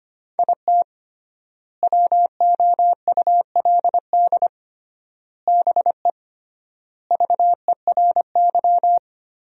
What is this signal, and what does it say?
Morse code, heard as a keyed tone, about 25 wpm, 700 Hz. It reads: IT WOULD BE VERY
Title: 3x Word-Spacing - Sentences from Top 200 Words - 25wpm